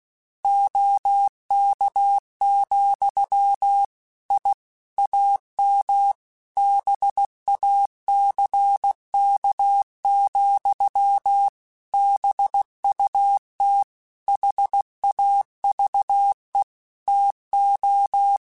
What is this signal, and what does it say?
Morse code, heard as a keyed tone, about 16 wpm, 800 Hz. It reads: OK, I AM BACK, BUT HAVE TO
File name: QSOs/QSOs-mp3s/kc7haa@15